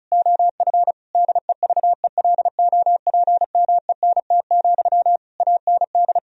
Morse code, heard as a keyed tone, about 36 wpm, 700 Hz. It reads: OF DEVELOPMENT, AND
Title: Sentences from Top 1000 Words - Review Part 4 - 35wpm